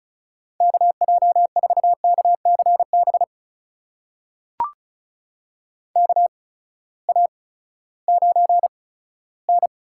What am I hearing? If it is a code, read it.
Morse code, heard as a keyed tone, about 36 wpm, spaced as for 18 wpm, 700 Hz. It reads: KJ4KCB KA9N
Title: USA Call Signs - 6 Characters - 8x Character Spacing - 35wpm